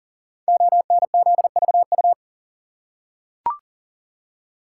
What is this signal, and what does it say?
Morse code, heard as a keyed tone, about 41 wpm, 700 Hz. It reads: ONZVU
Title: Code Groups - 5 Letter:Numbers - 40wpm